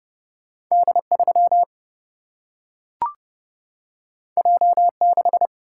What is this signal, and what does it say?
Morse code, heard as a keyed tone, about 30 wpm, 700 Hz. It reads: D3 J6
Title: Single-Letter Single-Number 30wpm